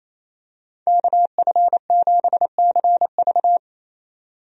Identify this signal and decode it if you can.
Morse code, heard as a keyed tone, about 28 wpm, 700 Hz. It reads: KF7CV
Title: USA Call Signs - 5 Characters - 28wpm